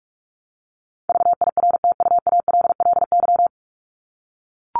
Morse code, heard as a keyed tone, about 44 wpm, 700 Hz. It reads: VIRTUALLY
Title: Top 1501-1750 Words 45wpm